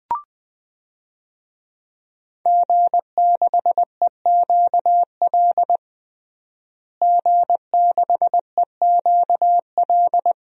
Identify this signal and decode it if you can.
Morse code, heard as a keyed tone, about 20 wpm, 700 Hz. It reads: G6EQL G6EQL
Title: DXCC-100 Call Signs - 5 Characters - Repeated 3x - 20wpm